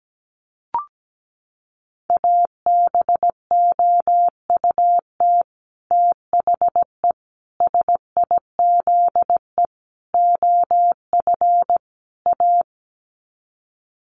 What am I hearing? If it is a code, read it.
Morse code, heard as a keyed tone, about 17 wpm, 700 Hz. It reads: ABOUT THE SIZE OF A
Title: Sets of 5 Words 17wpm